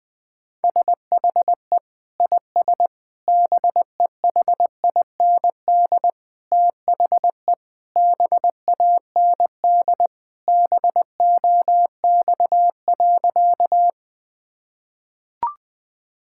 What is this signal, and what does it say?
Morse code, heard as a keyed tone, about 20 wpm, 700 Hz. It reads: SHE IS BEHIND THE BAND BOX.
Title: Sentences from Top 1600 Words 20wpm